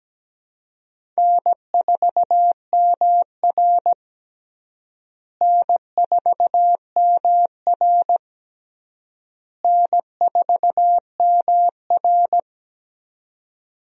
Morse code, heard as a keyed tone, about 17 wpm, 700 Hz. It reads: N4MR N4MR N4MR
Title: USA Call Signs - 4 Characters - Repeated 3x - 17wpm